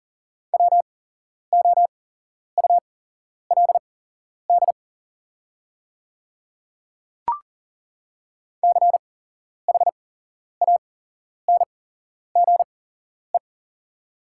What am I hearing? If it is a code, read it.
Morse code, heard as a keyed tone, about 40 wpm, spaced as for 11 wpm, 700 Hz. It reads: WOULD CHANGE
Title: 8x Character Spacing - Top 200 Words - 40wpm